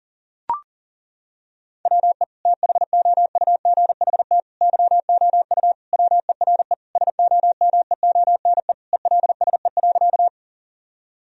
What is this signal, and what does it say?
Morse code, heard as a keyed tone, about 40 wpm, 700 Hz. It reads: WE THOUGHT YOU WERE SOMEONE ELSE.
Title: Sentences from Top 1000 Words - Review Part 4 - 40wpm